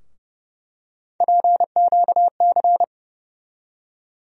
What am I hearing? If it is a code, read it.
Morse code, heard as a keyed tone, about 30 wpm, 700 Hz. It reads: PQC
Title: USA Call Sign Postfixes - 3 Letters 30wpm